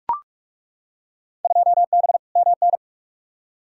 Morse code, heard as a keyed tone, about 42 wpm, 700 Hz. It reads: 2B MN
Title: Field Day Exchanges - 45wpm